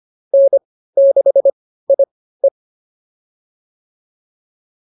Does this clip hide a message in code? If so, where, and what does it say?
Yes: Morse code from 0.3 to 2.5 s, heard as a keyed tone, about 25 wpm, spaced as for 15 wpm, 590 Hz. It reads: N6IE